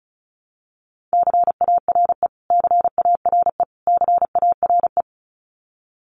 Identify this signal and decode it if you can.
Morse code, heard as a keyed tone, about 34 wpm, 700 Hz. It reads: CARE CARE CARE
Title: Top 500 Words Repeated 35wpm